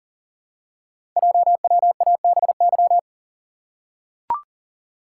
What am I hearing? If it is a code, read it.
Morse code, heard as a keyed tone, about 39 wpm, 690 Hz. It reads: JWABY
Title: Code Groups - 5 Letters - 40wpm